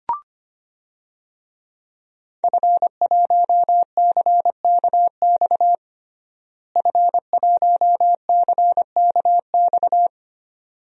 Morse code, heard as a keyed tone, about 25 wpm, 700 Hz. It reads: F1CKX F1CKX
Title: DXCC-100 Call Signs - 5 Characters - Repeated 2x - 25wpm